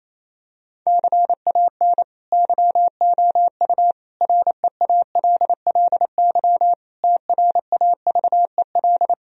Morse code, heard as a keyed tone, about 28 wpm, 700 Hz. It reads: CAN YOU REALLY TRAVEL
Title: Sentences from Top 1000 Words - Review Part 2 - 28wpm